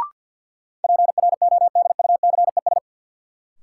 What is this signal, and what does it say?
Morse code, heard as a keyed tone, about 53 wpm, 700 Hz. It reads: PRODUCES
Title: Sentences from Top 1000 Words - Review Part 6 - New Word Drill - 50wpm